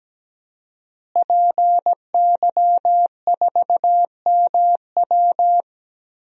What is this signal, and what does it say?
Morse code, heard as a keyed tone, about 17 wpm, 700 Hz. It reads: PY4MW
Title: DXCC-100 Call Signs - 5 Characters - Repeated 3x - 17wpm